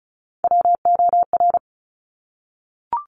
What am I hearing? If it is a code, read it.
Morse code, heard as a keyed tone, about 36 wpm, 700 Hz. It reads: WOR
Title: Top Three Letter Combinations in Words 35wpm